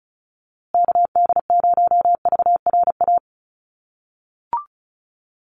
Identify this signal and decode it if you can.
Morse code, heard as a keyed tone, about 34 wpm, 700 Hz. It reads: KD0VRA
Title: USA Call Signs Repeated 2x 35wpm